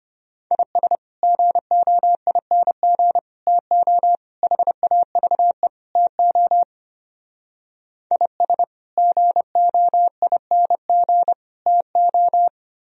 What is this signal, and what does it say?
Morse code, heard as a keyed tone, about 32 wpm, 700 Hz. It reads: IS GOING TO HAVE TO IS GOING TO